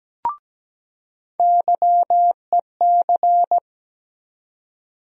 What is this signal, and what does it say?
Morse code, heard as a keyed tone, about 17 wpm, 700 Hz. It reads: YEC
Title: USA Call Sign Postfixes - 3 Letters 17wpm